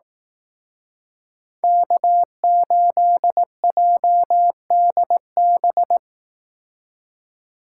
Morse code, heard as a keyed tone, about 19 wpm, 700 Hz. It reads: K8JDB
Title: Speed Racing - USA 1x3 Call Signs 15wpm